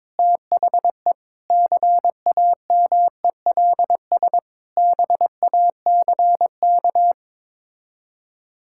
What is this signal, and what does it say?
Morse code, heard as a keyed tone, about 21 wpm, 700 Hz. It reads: THE CAMELS BACK
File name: practice / 22wpm / Sets of 8 Words - 22wpm